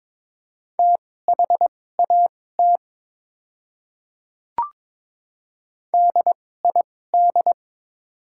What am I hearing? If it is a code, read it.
Morse code, heard as a keyed tone, about 21 wpm, spaced as for 16 wpm, 700 Hz. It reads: THAT DID
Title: 2x Character Spacing - Top 200 Words - 22wpm